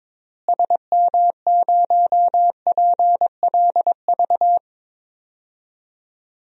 Morse code, heard as a keyed tone, about 22 wpm, 700 Hz. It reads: SM0PLV
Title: DXCC-100 Call Signs - 6 Characters - Repeated 2x - 22wpm